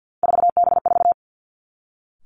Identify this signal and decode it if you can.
Morse code, heard as a keyed tone, about 50 wpm, 700 Hz. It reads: 4B4
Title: Three Challenging Letter-Number Combinations 50wpm